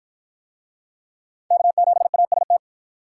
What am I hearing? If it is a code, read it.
Morse code, heard as a keyed tone, about 53 wpm, 700 Hz. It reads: K7AST